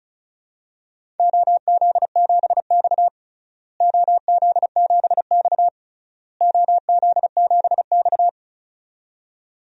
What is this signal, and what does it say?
Morse code, heard as a keyed tone, about 36 wpm, 700 Hz. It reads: OZ7X OZ7X OZ7X
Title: DXCC-100 Call Signs - 4 Characters - Repeated 3x - 35wpm